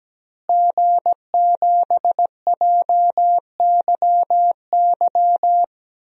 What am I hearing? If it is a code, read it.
Morse code, heard as a keyed tone, about 17 wpm, 700 Hz. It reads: G7JYY
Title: DXCC-100 Call Signs - 5 Characters - Repeated 2x - 17wpm